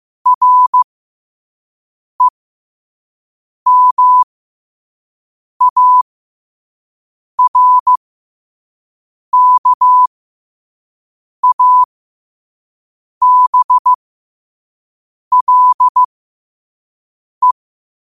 Morse code, heard as a keyed tone, about 15 wpm, spaced as for 5 wpm, 1000 Hz. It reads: REMARKABLE
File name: qotdfiles / 5wpm